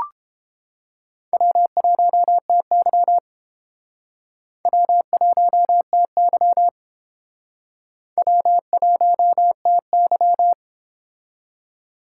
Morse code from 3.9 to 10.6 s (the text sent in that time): W1TY W1TY